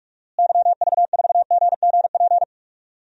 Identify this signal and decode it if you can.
Morse code, heard as a keyed tone, about 43 wpm, 700 Hz. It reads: YV4GGP
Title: DXCC-100 Call Signs - 6 Characters - Repeated 2x - 45wpm